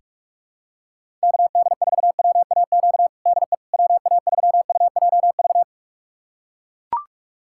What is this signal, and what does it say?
Morse code, heard as a keyed tone, about 41 wpm, 700 Hz. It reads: KD4WAQ DE WA3UJV
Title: USA Callsign DE Callsign 45wpm